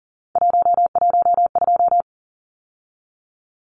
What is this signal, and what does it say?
Morse code, heard as a keyed tone, about 40 wpm, 700 Hz. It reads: 112
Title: Numbers - 3-Digit 40wpm